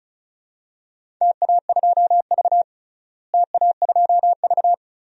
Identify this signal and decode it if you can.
Morse code, heard as a keyed tone, about 36 wpm, 700 Hz. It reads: TA2V TA2V
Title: DXCC-100 Call Signs - 4 Characters - Repeated 2x - 35wpm